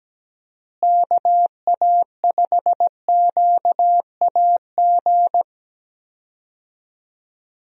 Morse code, heard as a keyed tone, about 17 wpm, 700 Hz. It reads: KA5QAG
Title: USA Call Signs - 6 Characters - Repeated 3x - 17wpm